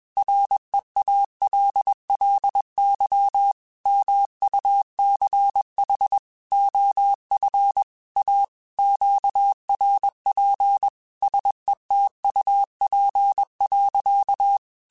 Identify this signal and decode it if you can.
Morse code, heard as a keyed tone, about 21 wpm, 800 Hz. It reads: REALLY MUCH OF A QRP SETUP.
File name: QSOs / QSOs-mp3s / WD0BGP 1@20